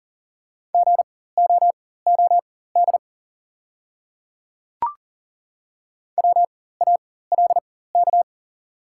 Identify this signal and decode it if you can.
Morse code, heard as a keyed tone, about 38 wpm, spaced as for 18 wpm, 700 Hz. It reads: GOOD WALK
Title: 4x Character Spacing - Top 300 Words - 40wpm